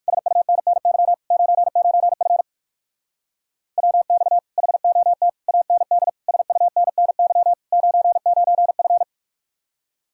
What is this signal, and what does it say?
Morse code, heard as a keyed tone, about 58 wpm, 700 Hz. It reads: SUNNY 99F WX HOT AND SUNNY 99F